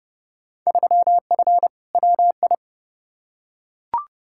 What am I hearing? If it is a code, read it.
Morse code, heard as a keyed tone, about 30 wpm, 700 Hz. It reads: 3F WI
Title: Field Day Exchanges - 30wpm